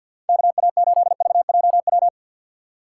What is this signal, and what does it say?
Morse code, heard as a keyed tone, about 51 wpm, 700 Hz. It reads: KA8VJW
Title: USA Call Signs - 6 Characters - 8x Character Spacing - 50wpm